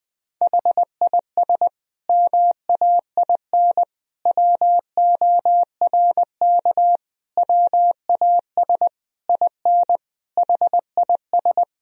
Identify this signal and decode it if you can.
Morse code, heard as a keyed tone, about 21 wpm, 700 Hz. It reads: HIS MAIN WORK WAS IN HIS
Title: Sentences from Top 1000 Words - Review Part 2 - 20wpm